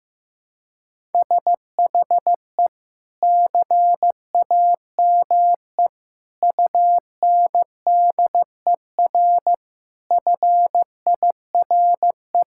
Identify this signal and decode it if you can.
Morse code, heard as a keyed tone, about 15 wpm, 700 Hz. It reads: SHE CAME UNDER FIRE
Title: Sentences from Top 1000 Words - Review Part 6 - 15wpm